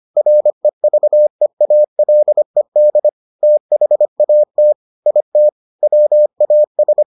Morse code, heard as a keyed tone, about 25 wpm, 600 Hz. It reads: REVEALED THAT IT WAS